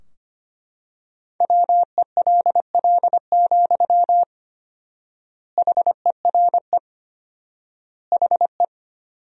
Morse code, heard as a keyed tone, about 25 wpm, 700 Hz. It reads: WELL, HERE HE
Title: 4x Word-Spacing - Sentences from Top 200 Words - 25wpm